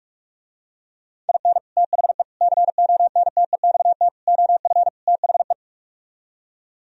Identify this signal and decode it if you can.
Morse code, heard as a keyed tone, about 44 wpm, 700 Hz. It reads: IN THE CONTEXT OF THE